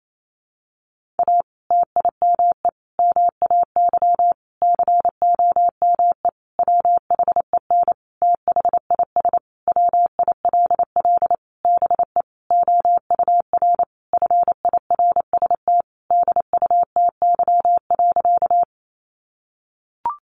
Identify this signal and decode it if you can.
Morse code, heard as a keyed tone, about 28 wpm, 700 Hz. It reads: A TIME MAY COME WHEN THIS WILL BE OUR FIRST DUTY.
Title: Sentences from Top 1500 Words 28wpm